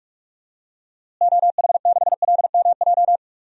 Morse code, heard as a keyed tone, about 41 wpm, 700 Hz. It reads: OH6LMJ
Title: DXCC-100 Call Signs - 6 Characters - Repeated 3x - 45wpm